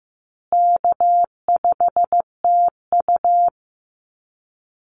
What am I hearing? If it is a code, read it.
Morse code, heard as a keyed tone, about 15 wpm, 700 Hz. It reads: K5TU
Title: Easy Call Signs Repeated 2x - 15wpm